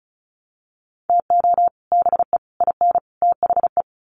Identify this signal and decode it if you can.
Morse code, heard as a keyed tone, about 34 wpm, 700 Hz. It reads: TO BE IN THE
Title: Sets of 4 Words 35wpm